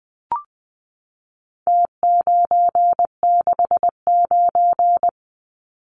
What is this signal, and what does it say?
Morse code, heard as a keyed tone, about 20 wpm, 700 Hz. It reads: T969
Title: Numbers - Contest Style - 4-Digit 20wpm